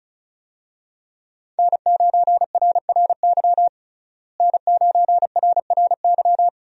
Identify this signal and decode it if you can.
Morse code, heard as a keyed tone, about 36 wpm, 700 Hz. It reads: N9RRY N9RRY
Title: USA Call Signs - 5 Characters - Repeated 2x - 35wpm